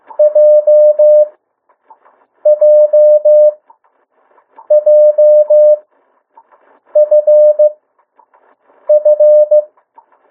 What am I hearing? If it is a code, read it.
Morse code, heard as a keyed tone, about 15 wpm, spaced as for 6 wpm, 600 Hz. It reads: JJJFF
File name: morse mp3 / 02 urok 1